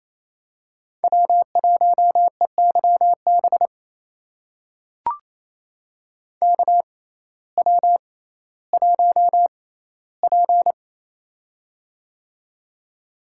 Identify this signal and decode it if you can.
Morse code, heard as a keyed tone, about 28 wpm, spaced as for 15 wpm, 700 Hz. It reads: W1EYB KW1P
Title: USA Call Signs - 6x Character Spacing - 28wpm